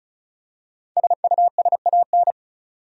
Morse code, heard as a keyed tone, about 36 wpm, 700 Hz. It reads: SUSAN
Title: USA Common Names - Max 5 Letters - 35wpm